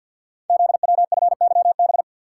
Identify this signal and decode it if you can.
Morse code, heard as a keyed tone, about 50 wpm, 690 Hz. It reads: ZWFYB
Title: Code Groups - 5 Letters - 50wpm